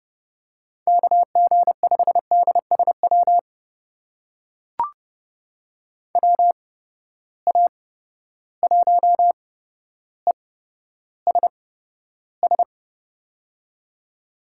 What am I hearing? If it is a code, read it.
Morse code, heard as a keyed tone, about 30 wpm, spaced as for 13 wpm, 700 Hz. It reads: KG5DSW WA1ESS
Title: USA Call Signs - 6 Characters - 8x Character Spacing - 30wpm